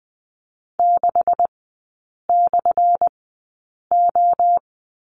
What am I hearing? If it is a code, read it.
Morse code, heard as a keyed tone, about 20 wpm, spaced as for 8 wpm, 700 Hz. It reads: 6/O
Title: 178 Lesson 29e - Slash - Three Characters 20wpm